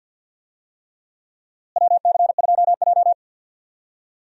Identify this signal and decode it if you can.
Morse code, heard as a keyed tone, about 48 wpm, 700 Hz. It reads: WC2J
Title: USA Call Signs - 4 Characters - Repeated 3x - 50wpm